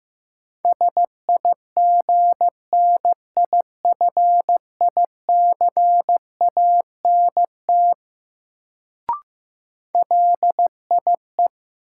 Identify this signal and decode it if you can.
Morse code, heard as a keyed tone, about 15 wpm, 700 Hz. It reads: SIGNIFICANT LIE
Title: Sentences from Top 1600 Words - New Word Drill Alt - 15wpm